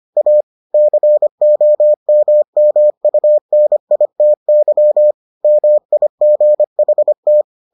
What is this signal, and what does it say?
Morse code, heard as a keyed tone, about 25 wpm, 600 Hz. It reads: A COMMUNITY MIGHT